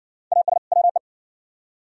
Morse code, heard as a keyed tone, about 60 wpm, 690 Hz. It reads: AS WE